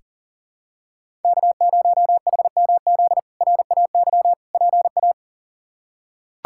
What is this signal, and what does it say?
Morse code, heard as a keyed tone, about 37 wpm, 700 Hz. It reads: K0HMZ RAY PA
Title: K1USN SST QSO Practice - 40wpm